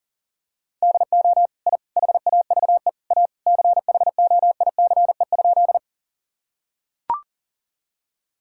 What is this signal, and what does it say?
Morse code, heard as a keyed tone, about 40 wpm, 700 Hz. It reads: DO I HAVE A CHOICE?
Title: Sentences from Top 1000 Words - Review Part 6 - 40wpm